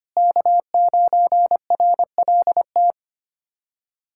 Morse code, heard as a keyed tone, about 25 wpm, 700 Hz. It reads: K9RLT
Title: USA Callsign DE Callsign - 5 Characters - 25wpm